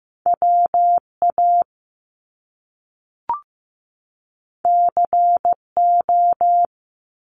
Morse code, heard as a keyed tone, about 15 wpm, 700 Hz. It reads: WA CO
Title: US States 15wpm